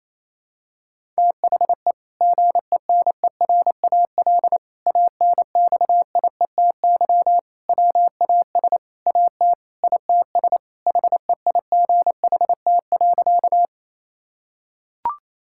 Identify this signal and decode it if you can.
Morse code, heard as a keyed tone, about 28 wpm, 700 Hz. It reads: THE GENERAL ANXIETY WAS AT ITS HEIGHT.
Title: Sentences from Top 1600 Words 28wpm